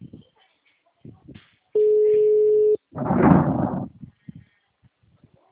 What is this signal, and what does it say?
Ringback tone (Europe, 425 Hz).
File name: out- -2013- - - .43729